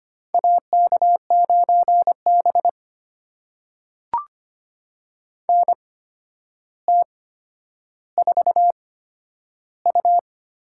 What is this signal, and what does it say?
Morse code, heard as a keyed tone, about 25 wpm, spaced as for 11 wpm, 700 Hz. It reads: AK9B NT4U
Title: USA Call Signs - 4 Characters - 8x Character Spacing - 25wpm